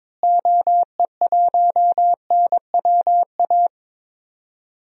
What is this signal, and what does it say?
Morse code, heard as a keyed tone, about 22 wpm, 700 Hz. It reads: OE1NWA
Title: DXCC-100 Call Signs - 6 Characters - Repeated 2x - 22wpm